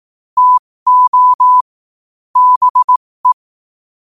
Morse code, heard as a keyed tone, about 18 wpm, spaced as for 15 wpm, 1000 Hz. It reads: TO BE
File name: qotdfiles / 15wpm